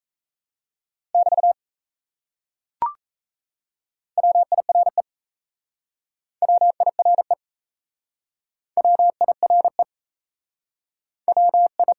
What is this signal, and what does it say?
Morse code, heard as a keyed tone, about 37 wpm, 700 Hz. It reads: <BT> WIRE WIRE WIRE WI